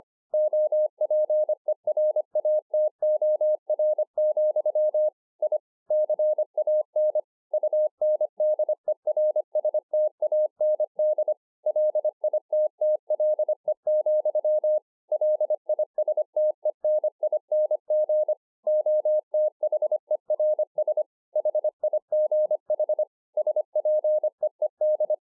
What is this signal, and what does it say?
Morse code, heard as a keyed tone, about 26 wpm, 610 Hz. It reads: OPERATOR, I CAN UNDERSTAND LITTLE, LISTENING OTHERS HIGH SPEED
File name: 2011-04-mp3 / BEGINNERS DIARY 25WPM